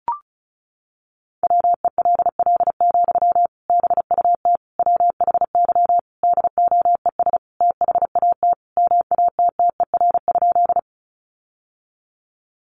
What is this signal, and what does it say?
Morse code, heard as a keyed tone, about 34 wpm, 700 Hz. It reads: WELL, BUT WHY DOES THAT MATTER?
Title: Sentences from Top 700 Words 35wpm